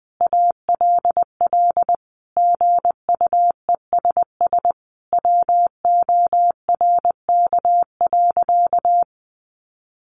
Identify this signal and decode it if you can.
Morse code, heard as a keyed tone, about 20 wpm, 700 Hz. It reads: ALL GUESS WORK.
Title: Sentences from Top 1500 Words 20wpm